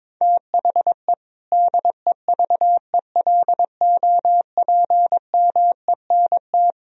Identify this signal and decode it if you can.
Morse code, heard as a keyed tone, about 21 wpm, 700 Hz. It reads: THE DEVELOPMENT
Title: Sentences from Top 1000 Words - Review Part 4 - 22wpm